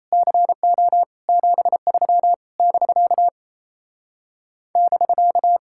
Morse code, heard as a keyed tone, about 28 wpm, 700 Hz. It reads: CO 73 <BK> <BK>
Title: Speed-Racing Common QSO Phrases - 28wpm